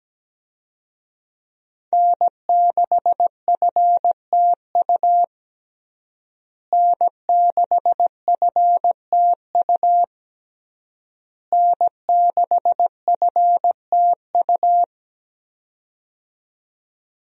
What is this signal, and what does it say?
Morse code, heard as a keyed tone, about 17 wpm, 700 Hz. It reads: N6FTU N6FTU N6FTU
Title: USA Call Signs - 5 Characters - Repeated 3x - 17wpm